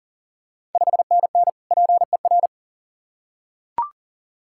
Morse code, heard as a keyed tone, about 39 wpm, 700 Hz. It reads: 5NN PER
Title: Ontario QSO Party Exchange 40wpm